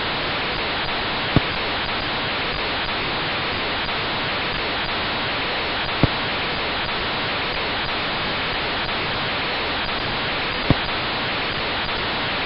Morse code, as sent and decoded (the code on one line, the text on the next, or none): none